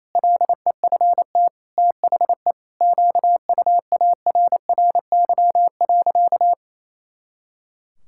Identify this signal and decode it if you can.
Morse code, heard as a keyed tone, about 28 wpm, 700 Hz. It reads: LEFT THE QUARRY.
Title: Pangram Challenge v2 - 28wpm